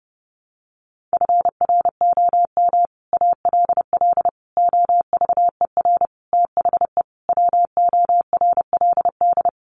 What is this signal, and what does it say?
Morse code, heard as a keyed tone, about 30 wpm, 700 Hz. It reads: FROM ALL OVER THE WORLD
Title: Sets of 5 Words 30wpm